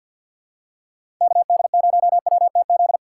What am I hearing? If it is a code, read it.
Morse code, heard as a keyed tone, about 49 wpm, 700 Hz. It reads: KD0WTZ
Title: USA Call Signs - 6 Characters - 6x Character Spacing - 50wpm